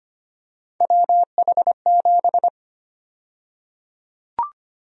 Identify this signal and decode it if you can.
Morse code, heard as a keyed tone, about 25 wpm, 700 Hz. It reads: WH7
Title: USA Call Sign Prefixes - 25wpm